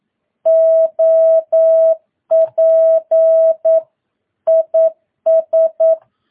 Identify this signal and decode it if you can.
Morse code, heard as a keyed tone, about 9 wpm, 650 Hz. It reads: OP IS
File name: jh8cbh / 2022cw